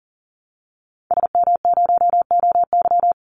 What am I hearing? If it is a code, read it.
Morse code, heard as a keyed tone, about 40 wpm, 700 Hz. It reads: SM0OY
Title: Call Signs Repeated 2x - 40wpm